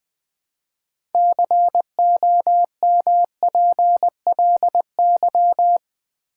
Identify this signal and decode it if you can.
Morse code, heard as a keyed tone, about 20 wpm, 700 Hz. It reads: COMPLY
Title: Build A Word - 3 Letters At A Time - 6-Letter Words - 20wpm